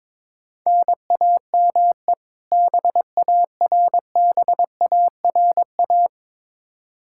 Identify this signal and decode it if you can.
Morse code, heard as a keyed tone, about 22 wpm, 700 Hz. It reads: NAME BARBARA
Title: Practice QSO - Exchange 1 - Beginning - 22wpm